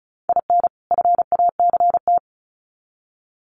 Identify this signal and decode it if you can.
Morse code, heard as a keyed tone, about 34 wpm, 700 Hz. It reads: IN FACT
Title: Sets of 2 Words 35wpm